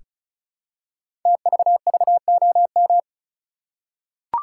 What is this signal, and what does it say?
Morse code, heard as a keyed tone, about 36 wpm, 700 Hz. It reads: TVVOM
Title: Code Groups - 5 Letters - 35wpm